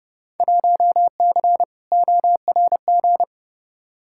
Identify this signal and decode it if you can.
Morse code, heard as a keyed tone, about 30 wpm, 700 Hz. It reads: 1C ORG
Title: Field Day Exchanges - 30wpm